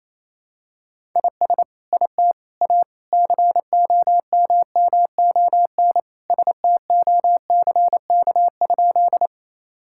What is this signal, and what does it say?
Morse code, heard as a keyed tone, about 28 wpm, 700 Hz. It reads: IS IT A COMMON STOCK?
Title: Sentences from Top 1000 Words - Review Part 5 - 28wpm